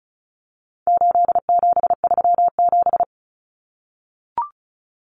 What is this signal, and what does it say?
Morse code, heard as a keyed tone, about 34 wpm, 700 Hz. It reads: 8737
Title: Common Cut Numbers - 4-Digit 35wpm